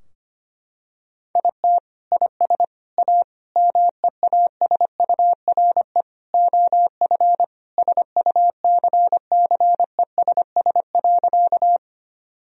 Morse code, heard as a keyed tone, about 25 wpm, 700 Hz. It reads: IT IS A MEASURE OF SUCCESS.
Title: Sentences from Top 1000 Words - Review Part 2 - 25wpm